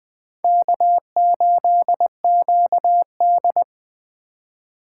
Morse code, heard as a keyed tone, about 21 wpm, 700 Hz. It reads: K8QD
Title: Speed Racing - USA Call Signs 17wpm